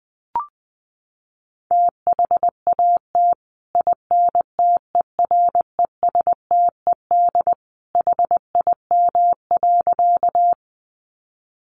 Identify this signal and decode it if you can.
Morse code, heard as a keyed tone, about 20 wpm, 700 Hz. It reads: THAT INTERESTED HIM.
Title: Sentences from Top 900 Words 20wpm